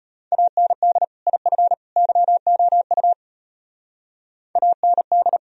Morse code, heard as a keyed tone, about 31 wpm, 700 Hz. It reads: AND IF YOU AND
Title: Speed-Racing - Sets of 3 Words 25wpm